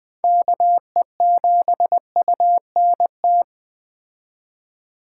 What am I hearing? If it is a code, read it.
Morse code, heard as a keyed tone, about 20 wpm, 700 Hz. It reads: KE7UNT
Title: Speed Racing - USA 2x3 Call Signs 17wpm